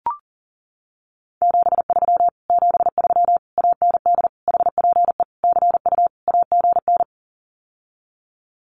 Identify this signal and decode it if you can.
Morse code, heard as a keyed tone, about 37 wpm, 700 Hz. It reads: 73 73 AND HPE CU AGN
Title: Common QSO Phrases Encore 40wpm